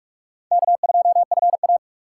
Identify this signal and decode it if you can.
Morse code, heard as a keyed tone, about 41 wpm, 700 Hz. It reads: K2FA
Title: USA Call Signs - 4 Characters - 3x Character Spacing - 45wpm